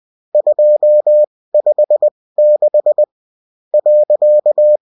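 Morse code, heard as a keyed tone, about 20 wpm, spaced as for 16 wpm, 600 Hz. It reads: 256 .
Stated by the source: Morse code bulletin of a French amateur CW union, sent as text and in CW.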